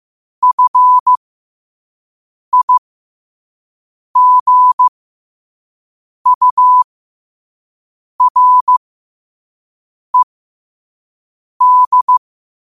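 Morse code, heard as a keyed tone, about 15 wpm, spaced as for 5 wpm, 1000 Hz. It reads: FIGURED
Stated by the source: Morse code quote practice at 5 words per minute.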